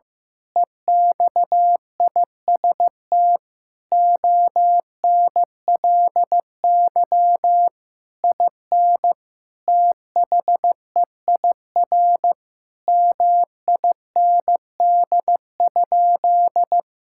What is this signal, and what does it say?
Morse code, heard as a keyed tone, about 15 wpm, 700 Hz. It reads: EXIST ONLY IN THEIR MIND?
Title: Sentences from Top 1000 Words - Review Part 5 - 15wpm